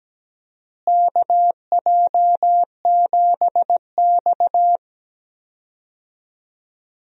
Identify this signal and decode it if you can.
Morse code, heard as a keyed tone, about 17 wpm, 700 Hz. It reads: KJ7X
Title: USA Call Signs - 4 Characters - Repeated 3x - 17wpm